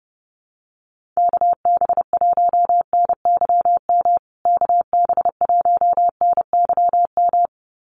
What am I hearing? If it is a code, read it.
Morse code, heard as a keyed tone, about 28 wpm, 700 Hz. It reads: KB1NYM KB1NYM
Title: USA Call Signs Repeated 2x 30wpm